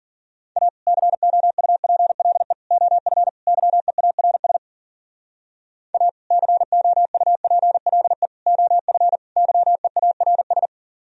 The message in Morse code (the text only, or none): A COUPLE OF YEARS A COUPLE OF YEARS